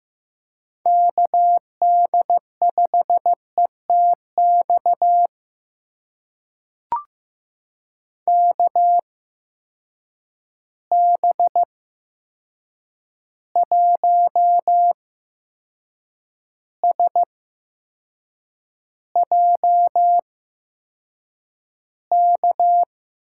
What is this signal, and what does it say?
Morse code, heard as a keyed tone, about 15 wpm, spaced as for 6 wpm, 700 Hz. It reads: KD5ETX KB1SJK
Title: USA Call Signs - 6 Characters - 8x Character Spacing - 15wpm